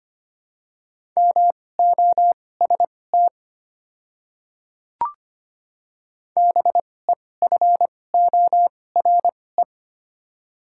Morse code, heard as a keyed tone, about 25 wpm, spaced as for 18 wpm, 700 Hz. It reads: MOST BEFORE
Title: 2x Character Spacing - Top 200 Words - 25wpm